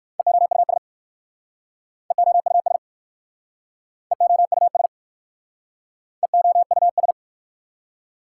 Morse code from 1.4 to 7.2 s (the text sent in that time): EOUS EOUS EOUS